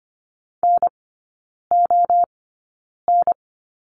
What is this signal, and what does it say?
Morse code, heard as a keyed tone, about 25 wpm, spaced as for 9 wpm, 690 Hz. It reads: NON
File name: Lesson 15e - Number 5 - Three Characters 25wpm